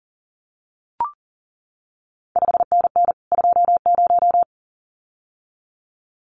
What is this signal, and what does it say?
Morse code, heard as a keyed tone, about 40 wpm, 700 Hz. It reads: 5NN 20
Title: CQ WW DX Contest Exchange Encore - 40wpm